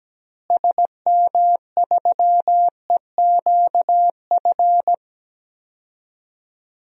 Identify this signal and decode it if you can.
Morse code, heard as a keyed tone, about 17 wpm, 700 Hz. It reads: SM3EQF
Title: DXCC-100 Call Signs - 6 Characters - Repeated 3x - 17wpm